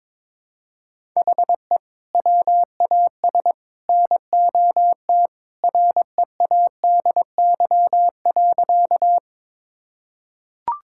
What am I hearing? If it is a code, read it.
Morse code, heard as a keyed tone, about 22 wpm, 700 Hz. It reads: HE WAS NOT READY.
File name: practice / 22wpm / Sentences with 3-Word Phrases - 22wpm